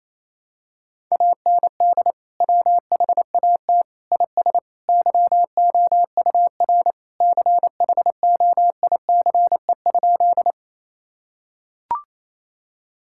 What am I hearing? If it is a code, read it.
Morse code, heard as a keyed tone, about 27 wpm, 700 Hz. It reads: AND WHAT IS YOUR CHOICE?
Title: Sentences from Top 1000 Words - Review Part 6 - 28wpm